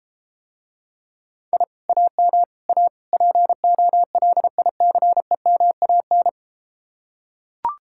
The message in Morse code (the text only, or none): I AM A POLICEMAN